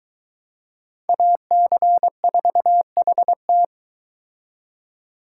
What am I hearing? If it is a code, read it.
Morse code, heard as a keyed tone, about 22 wpm, 700 Hz. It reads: AC4HT